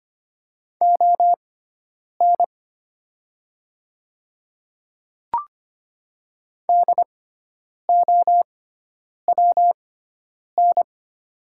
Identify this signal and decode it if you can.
Morse code, heard as a keyed tone, about 25 wpm, spaced as for 9 wpm, 700 Hz. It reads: ON DOWN
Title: 6x Character Spacing - Top 200 Words - 25wpm